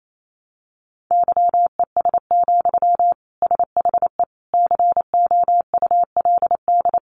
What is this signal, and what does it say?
Morse code, heard as a keyed tone, about 28 wpm, 700 Hz. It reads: YES, SHE COULD
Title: Sentences from Top 700 Words 28wpm